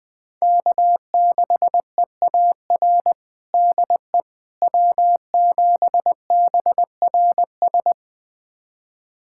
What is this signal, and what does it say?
Morse code, heard as a keyed tone, about 21 wpm, 700 Hz. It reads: K6EAR DE W7BRS
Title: USA Callsign DE Callsign - 5 Characters - 20wpm